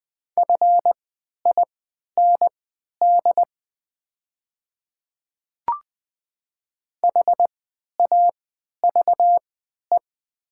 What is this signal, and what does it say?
Morse code, heard as a keyed tone, about 20 wpm, spaced as for 11 wpm, 700 Hz. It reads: FIND HAVE
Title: 3x Character Spacing - Top 100 Words - 20wpm